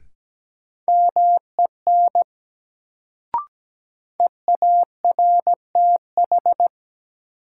Morse code, heard as a keyed tone, about 17 wpm, 700 Hz. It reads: MEN EARTH
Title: Sentences from Top 200 Words - New Word Drill - 17wpm